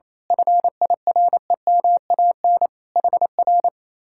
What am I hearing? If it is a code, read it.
Morse code, heard as a keyed tone, about 27 wpm, 700 Hz. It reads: FIREMAN HR
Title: Speed-Racing Common QSO Phrases - 28wpm